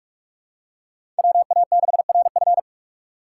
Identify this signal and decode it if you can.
Morse code, heard as a keyed tone, about 43 wpm, 700 Hz. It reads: WA6RF
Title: USA Call Signs - 4x Character Spacing - 45wpm